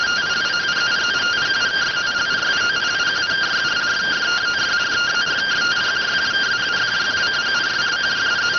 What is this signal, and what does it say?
Two-tone teleprinter signal, about 170 Hz shift.